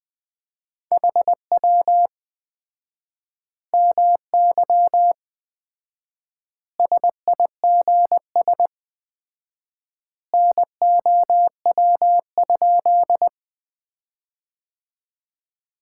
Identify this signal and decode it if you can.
Morse code, heard as a keyed tone, about 20 wpm, 700 Hz. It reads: HW MY SIGS NOW?
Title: Common QSO Phrases - 4x Word-Spacing - 20wpm